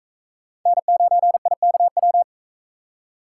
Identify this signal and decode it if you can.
Morse code, heard as a keyed tone, about 40 wpm, 700 Hz. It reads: N9IKW